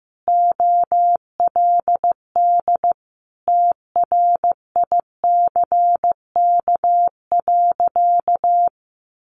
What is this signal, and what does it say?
Morse code, heard as a keyed tone, about 15 wpm, 700 Hz. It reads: OLD TRICK.
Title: Sentences from Top 1500 Words 15wpm